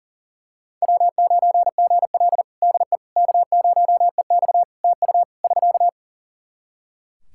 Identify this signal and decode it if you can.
Morse code, heard as a keyed tone, about 37 wpm, 700 Hz. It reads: W9GL DE K0EX TU <SK>
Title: Practice QSO - Exchange 4 - Ending - 40wpm